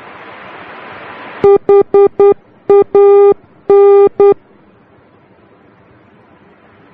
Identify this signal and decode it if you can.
Morse code, heard as a keyed tone, about 10 wpm, 400 Hz. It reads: HAN
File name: germany / mx / HAN-D